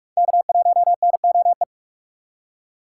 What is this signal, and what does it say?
Morse code, heard as a keyed tone, about 43 wpm, 700 Hz. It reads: K1NOE
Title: USA Call Signs - 5 Characters - 6x Character Spacing - 45wpm